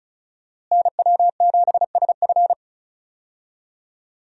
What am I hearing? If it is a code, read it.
Morse code, heard as a keyed tone, about 36 wpm, 700 Hz. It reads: NW7SF
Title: USA Call Signs - 5 Characters - Repeated 3x - 35wpm